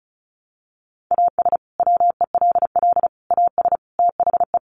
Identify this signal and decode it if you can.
Morse code, heard as a keyed tone, about 34 wpm, 700 Hz. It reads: AS WELL AS THE
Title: Sets of 4 Words 35wpm